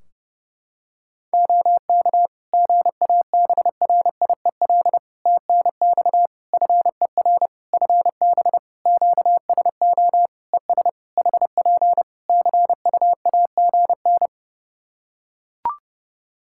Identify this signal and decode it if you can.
Morse code, heard as a keyed tone, about 32 wpm, 700 Hz. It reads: OK GABRIEL TNX FER FB QSO ES HP CUAGN
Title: Practice QSO - Exchange 4 - Ending - 30wpm